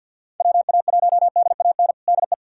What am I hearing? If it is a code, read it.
Morse code, heard as a keyed tone, about 47 wpm, 700 Hz. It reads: WA1DAN DE
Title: USA Callsign DE Callsign 50wpm